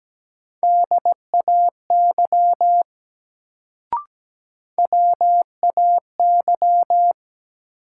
Morse code, heard as a keyed tone, about 17 wpm, 700 Hz. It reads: DAY WAY
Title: Sets of 2 Words - New Word Drill - 17wpm